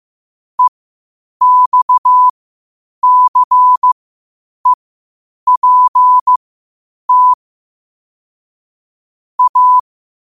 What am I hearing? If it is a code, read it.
Morse code, heard as a keyed tone, about 15 wpm, spaced as for 8 wpm, 1000 Hz. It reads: EXCEPT A
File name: qotdfiles / 8wpm